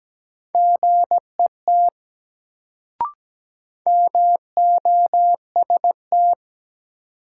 Morse code, heard as a keyed tone, about 17 wpm, 700 Hz. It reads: GET MOST
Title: Sentences from Top 200 Words - New Word Drill - 17wpm